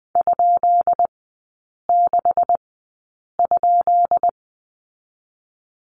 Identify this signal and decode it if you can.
Morse code, heard as a keyed tone, about 20 wpm, spaced as for 8 wpm, 690 Hz. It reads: ?6?
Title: 133 Lesson 22e - Question Mark - Three Characters 20wpm